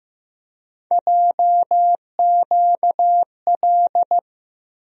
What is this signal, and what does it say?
Morse code, heard as a keyed tone, about 15 wpm, 700 Hz. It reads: JQL
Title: USA Call Sign Postfixes - 3 Letters 15wpm